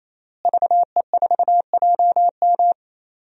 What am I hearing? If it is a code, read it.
Morse code, heard as a keyed tone, about 28 wpm, 700 Hz. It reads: VE4JM
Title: DXCC-100 Call Signs - 5 Characters - Repeated 3x - 28wpm